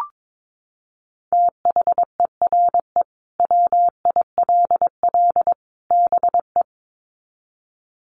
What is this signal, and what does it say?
Morse code, heard as a keyed tone, about 22 wpm, 700 Hz. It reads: THERE WILL BE
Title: Sets of 3 Words 22wpm